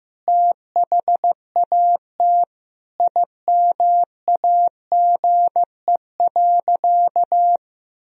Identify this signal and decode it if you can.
Morse code, heard as a keyed tone, about 16 wpm, 700 Hz. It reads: THAT IMAGE.
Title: Sentences from Top 1000 Words - Review Part 4 - 15wpm